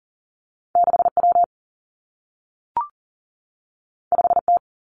Morse code, heard as a keyed tone, about 42 wpm, 700 Hz. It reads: 6 W 5 T
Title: Two Challenging Letter-Number Combinations 40wpm